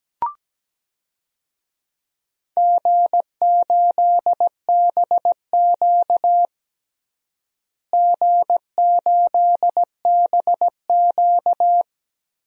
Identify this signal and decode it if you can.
Morse code, heard as a keyed tone, about 17 wpm, 700 Hz. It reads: G8BQ G8BQ
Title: DXCC-100 Call Signs - 4 Characters - Repeated 3x - 17wpm